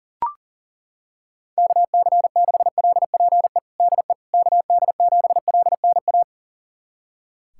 Morse code, heard as a keyed tone, about 40 wpm, 700 Hz. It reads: KC6LPE DE KD7LNA
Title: USA Callsign DE Callsign - 6 Characters - 40wpm